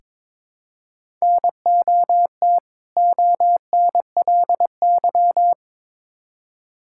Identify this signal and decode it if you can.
Morse code, heard as a keyed tone, about 22 wpm, 700 Hz. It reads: NOT ONLY
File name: Build A Phrase - 2-Word Phrases - 22wpm